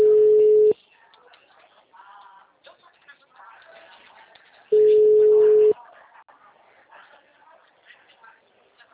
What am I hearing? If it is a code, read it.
Ringback tone (Europe, 425 Hz), with speech.